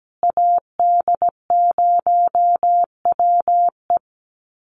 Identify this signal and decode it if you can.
Morse code, heard as a keyed tone, about 17 wpm, 700 Hz. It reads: AD0WE
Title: Call Signs Repeated 2x - 17wpm